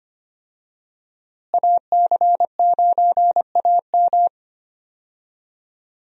Morse code, heard as a keyed tone, about 25 wpm, 700 Hz. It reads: AC9AM
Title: USA Call Signs - 5 Characters - Repeated 2x - 25wpm